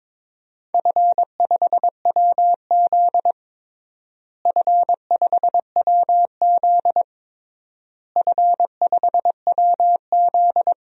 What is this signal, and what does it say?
Morse code, heard as a keyed tone, about 22 wpm, 700 Hz. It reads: F5WZ F5WZ F5WZ
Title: DXCC-100 Call Signs - 4 Characters - Repeated 3x - 22wpm